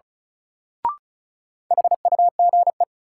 Morse code, heard as a keyed tone, about 34 wpm, 700 Hz. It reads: HUGE
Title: Sentences from Top 1100 Words - New Word Drill - 35wpm